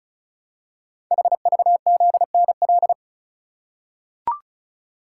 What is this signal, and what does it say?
Morse code, heard as a keyed tone, about 36 wpm, 700 Hz. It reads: HVZNL
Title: Code Groups - 5 Letters - 35wpm